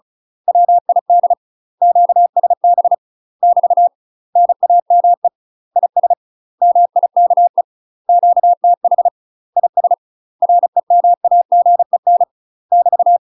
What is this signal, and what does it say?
Morse code, heard as a keyed tone, about 34 wpm, 700 Hz. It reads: WID QSB <BT> NAME IS MIKE QTH IS REMAGEN <BT>
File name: UK MASTER QSOs TRAINING / 35WPM / UK MASTER QSOs 0025